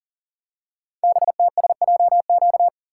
Morse code, heard as a keyed tone, about 40 wpm, 700 Hz. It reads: BTSJQ
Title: Code Groups - 5 Letters - 40wpm